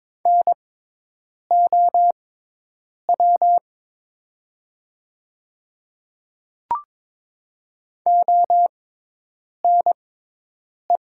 Morse code, heard as a keyed tone, about 22 wpm, spaced as for 8 wpm, 700 Hz. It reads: NOW ONE
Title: 6x Character Spacing - Top 100 Words - 22wpm